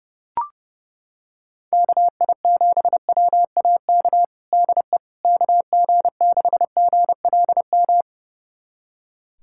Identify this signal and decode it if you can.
Morse code, heard as a keyed tone, about 32 wpm, 700 Hz. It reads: KI7WAK DE KG6GLM
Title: USA Callsign DE Callsign - 6 Characters - 30wpm